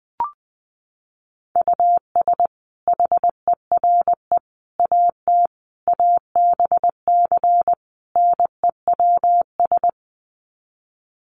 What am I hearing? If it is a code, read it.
Morse code, heard as a keyed tone, about 20 wpm, 700 Hz. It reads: US HERE AT ABC NEWS
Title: Sets of 5 Words 20wpm